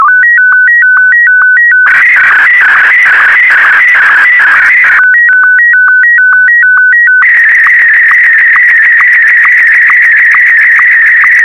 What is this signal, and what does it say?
SSTV (slow-scan television), Martin 1.